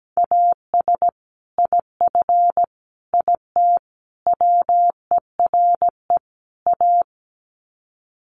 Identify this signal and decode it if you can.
Morse code, heard as a keyed tone, about 17 wpm, 700 Hz. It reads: AS IF IT WERE A
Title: Sets of 5 Words 17wpm